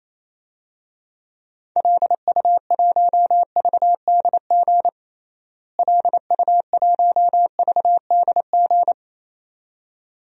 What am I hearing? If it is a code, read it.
Morse code, heard as a keyed tone, about 28 wpm, 700 Hz. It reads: LU1VDG LU1VDG
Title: DXCC-100 Call Signs - 6 Characters - Repeated 2x - 28wpm